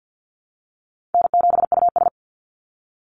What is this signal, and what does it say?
Morse code, heard as a keyed tone, about 49 wpm, 700 Hz. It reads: N7US
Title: Call Signs Repeated 2x - 50wpm